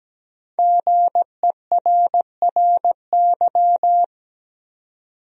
Morse code, heard as a keyed tone, about 17 wpm, 700 Hz. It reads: GERRY
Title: USA Common Names - Max 5 Letters - 17wpm